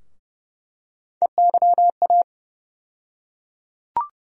Morse code, heard as a keyed tone, about 30 wpm, 700 Hz. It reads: EYA
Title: USA Call Sign Postfixes - 3 Letters 30wpm